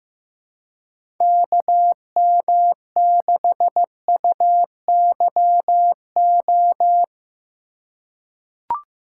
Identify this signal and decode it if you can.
Morse code, heard as a keyed tone, about 15 wpm, 700 Hz. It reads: KM6UYO
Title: USA Call Signs - 2x Character Spacing - 15wpm